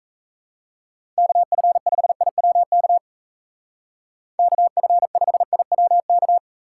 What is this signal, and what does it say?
Morse code, heard as a keyed tone, about 35 wpm, 700 Hz. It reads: KF5IWK KF5IWK
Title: Speed Racing - USA 2x3 Call Signs 28wpm